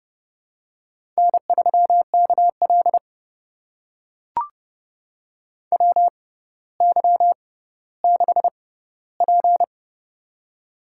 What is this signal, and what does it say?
Morse code, heard as a keyed tone, about 28 wpm, spaced as for 15 wpm, 700 Hz. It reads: N3KL WY6P
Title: USA Call Signs - 4 Characters - 6x Character Spacing - 30wpm